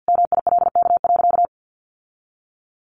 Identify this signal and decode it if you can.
Morse code, heard as a keyed tone, about 50 wpm, 700 Hz. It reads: MILK.
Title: Sentences from Top 1500 Words 50wpm